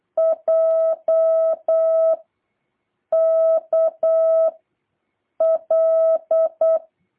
Morse code, heard as a keyed tone, about 8 wpm, 650 Hz. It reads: J K L